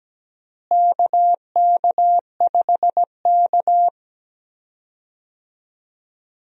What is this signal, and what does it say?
Morse code, heard as a keyed tone, about 17 wpm, 700 Hz. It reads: KK5K
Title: USA Call Signs - 4 Characters - Repeated 2x - 17wpm